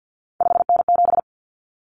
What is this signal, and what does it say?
Morse code, heard as a keyed tone, about 50 wpm, 700 Hz. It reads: 5N7
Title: Common Cut Numbers - 3-Digit 50wpm